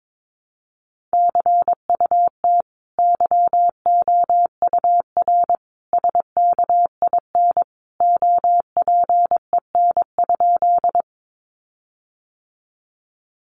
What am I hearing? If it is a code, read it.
Morse code, heard as a keyed tone, about 22 wpm, 700 Hz. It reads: CUT YOUR SKIN OPEN?
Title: Sentences from Top 1300 Words 22wpm